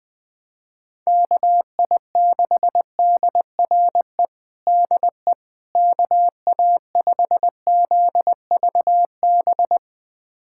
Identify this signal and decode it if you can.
Morse code, heard as a keyed tone, about 21 wpm, 700 Hz. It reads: KI6DRE DE KA5ZVB
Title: USA Callsign DE Callsign 20wpm